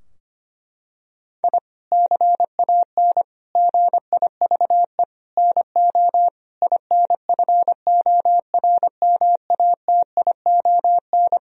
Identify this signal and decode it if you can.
Morse code, heard as a keyed tone, about 25 wpm, 700 Hz. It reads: I CAN GIVE NO INFORMATION
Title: Sentences from Top 1000 Words - Review Part 3 - 25wpm